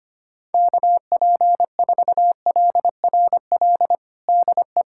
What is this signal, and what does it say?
Morse code, heard as a keyed tone, about 25 wpm, 700 Hz. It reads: KP4LRL DE
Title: USA Callsign DE Callsign 25wpm